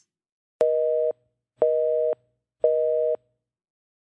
Busy tone (North America), 480 + 620 Hz.